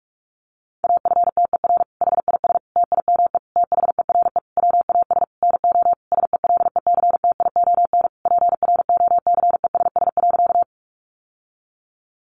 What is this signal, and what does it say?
Morse code, heard as a keyed tone, about 44 wpm, 700 Hz. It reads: AFTER HIS TIME THERE WAS NO SELECTION PROCESS.
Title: Sentences from Top 1400 Words 45wpm